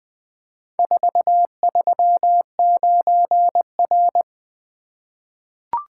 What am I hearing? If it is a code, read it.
Morse code, heard as a keyed tone, about 20 wpm, 700 Hz. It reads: 439R
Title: Four Letter-Number Combinations - 20wpm